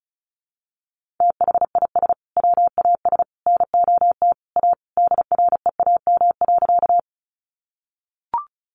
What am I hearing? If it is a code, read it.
Morse code, heard as a keyed tone, about 34 wpm, 700 Hz. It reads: THIS WAS NOT A DREAM.
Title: Sentences from Top 1500 Words 35wpm